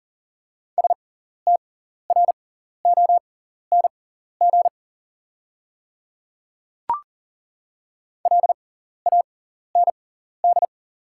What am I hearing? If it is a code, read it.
Morse code, heard as a keyed tone, about 40 wpm, spaced as for 14 wpm, 690 Hz. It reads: STRONG LAND
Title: 6x Character Spacing - Top 500 Words - 40wpm